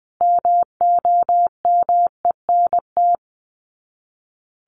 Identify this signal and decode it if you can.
Morse code, heard as a keyed tone, about 20 wpm, 700 Hz. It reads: MOMENT
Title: Sets of 3 Words 20wpm